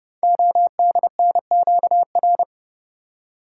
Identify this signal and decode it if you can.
Morse code, heard as a keyed tone, about 31 wpm, 700 Hz. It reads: ODNQR
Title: Code Groups - 5 Letters - 30wpm